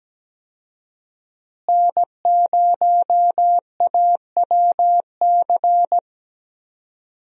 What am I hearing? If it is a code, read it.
Morse code, heard as a keyed tone, about 17 wpm, 700 Hz. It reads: N0AWC
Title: USA Call Signs - 5 Characters - Repeated 2x - 17wpm